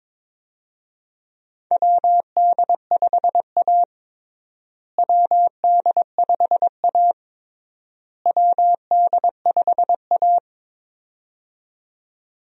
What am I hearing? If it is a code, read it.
Morse code, heard as a keyed tone, about 22 wpm, 700 Hz. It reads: WD5A WD5A WD5A
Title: USA Call Signs - 4 Characters - Repeated 3x - 22wpm